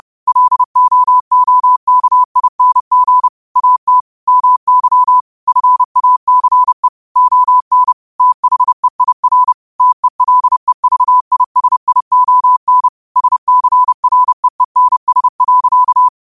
Morse code, heard as a keyed tone, about 30 wpm, 1000 Hz. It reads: LOOKING AT MY FACE ON THEIR TELEVISION SCREENS.
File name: 30wpm